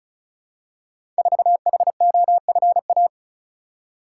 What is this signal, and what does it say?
Morse code, heard as a keyed tone, about 33 wpm, 700 Hz. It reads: 4HOFA
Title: Code Groups - 5 Letter:Numbers - 35wpm